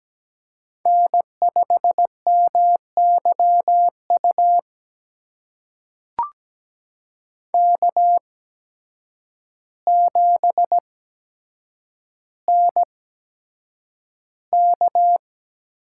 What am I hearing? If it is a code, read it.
Morse code, heard as a keyed tone, about 17 wpm, spaced as for 8 wpm, 700 Hz. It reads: N5MYU K7NK
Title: USA Call Signs - 8x Character Spacing - 17wpm